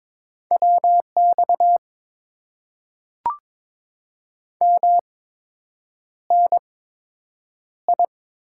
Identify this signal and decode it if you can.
Morse code, heard as a keyed tone, about 22 wpm, spaced as for 8 wpm, 700 Hz. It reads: WX MNI
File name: Top 90 QSO Elements - 8x Character Spacing - 22wpm